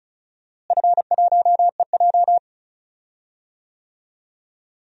Morse code, heard as a keyed tone, about 36 wpm, 700 Hz. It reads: F1EJ
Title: DXCC-100 Call Signs - 4 Characters - Repeated 2x - 35wpm